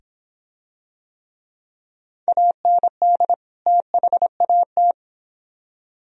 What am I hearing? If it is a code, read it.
Morse code, heard as a keyed tone, about 23 wpm, 700 Hz. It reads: AND THAT